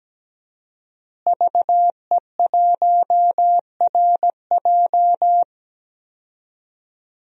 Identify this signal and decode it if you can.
Morse code, heard as a keyed tone, about 17 wpm, 700 Hz. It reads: VE1RJ
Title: DXCC-100 Call Signs - 5 Characters - Repeated 3x - 17wpm